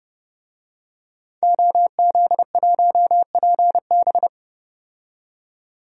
Morse code, heard as a keyed tone, about 30 wpm, 700 Hz. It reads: OZ1PB
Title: DXCC-100 Call Signs - 5 Characters - 30wpm